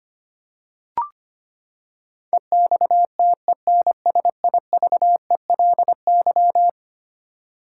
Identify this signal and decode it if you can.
Morse code, heard as a keyed tone, about 25 wpm, 700 Hz. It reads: EXTENSIVELY
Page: morse-code-ninja-practice-25wpm